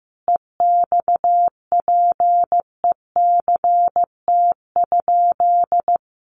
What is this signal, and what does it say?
Morse code, heard as a keyed tone, about 14 wpm, 700 Hz. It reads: EXPECT?
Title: Sentences from Top 1100 Words 15wpm